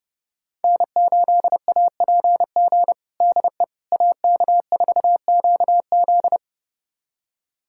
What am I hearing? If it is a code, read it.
Morse code, heard as a keyed tone, about 29 wpm, 700 Hz. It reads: N8APG DE AK4QZ
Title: USA Callsign DE Callsign - 5 Characters - 30wpm